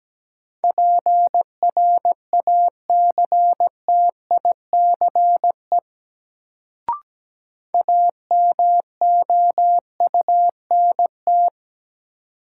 Morse code, heard as a keyed tone, about 17 wpm, 700 Hz. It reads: PRACTICE AMOUNT
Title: Sentences from Top 700 Words - New Word Drill - 17wpm